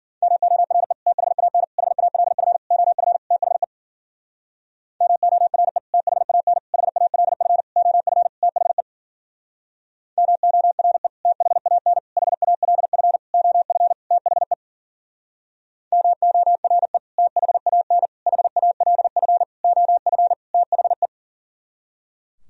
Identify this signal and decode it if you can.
Morse code, heard as a keyed tone, about 52 wpm, 700 Hz. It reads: MORE THAN HALF OF THE MORE THAN HALF OF THE MORE THAN HALF OF THE MORE THAN HALF OF THE